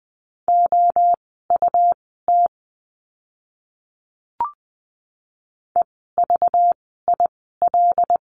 Morse code, heard as a keyed tone, about 20 wpm, spaced as for 15 wpm, 700 Hz. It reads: OUT EVIL
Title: 192 Lesson 31f - Letter v - Words 20wpm